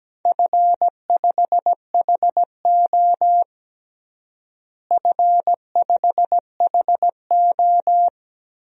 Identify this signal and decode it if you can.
Morse code, heard as a keyed tone, about 17 wpm, 700 Hz. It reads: F5HO F5HO
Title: DXCC-100 Call Signs - 4 Characters - Repeated 2x - 17wpm